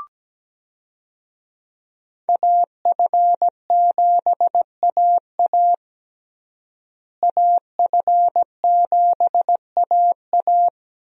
Morse code, heard as a keyed tone, about 17 wpm, 700 Hz. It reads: AF7AA AF7AA
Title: USA Call Signs - 5 Characters - Repeated 3x - 17wpm